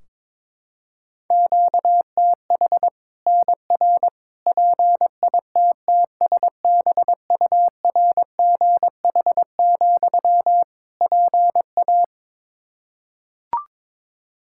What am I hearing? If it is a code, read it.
Morse code, heard as a keyed tone, about 22 wpm, 700 Hz. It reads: QTH NR PITTSBURGH, PA
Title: Practice QSO - Exchange 1 - Beginning - 22wpm